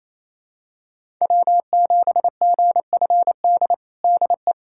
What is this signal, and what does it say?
Morse code, heard as a keyed tone, about 27 wpm, 700 Hz. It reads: W7GFD DE
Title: USA Callsign DE Callsign - 5 Characters - 28wpm